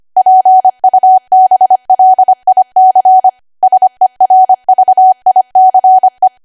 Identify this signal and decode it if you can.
Morse code, heard as a keyed tone, about 25 wpm, 750 Hz. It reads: PUBLIC SERVICE
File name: Archive / 25 WPM / 25WPM